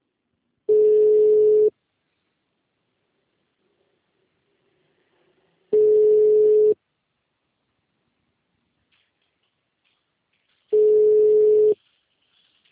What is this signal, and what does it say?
Ringback tone (Europe, 425 Hz).